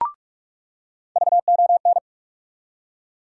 Morse code, heard as a keyed tone, about 46 wpm, 700 Hz. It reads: VON
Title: USA Call Sign Postfixes - 3 Letters 45wpm